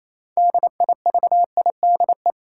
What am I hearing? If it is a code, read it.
Morse code, heard as a keyed tone, about 28 wpm, 700 Hz. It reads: DIVIDE
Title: Sentences from Top 1600 Words - New Word Drill - 28wpm